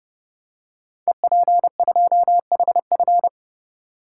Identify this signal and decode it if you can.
Morse code, heard as a keyed tone, about 30 wpm, 700 Hz. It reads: EP2HF
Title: Code Groups - 5 Letter:Numbers - 30wpm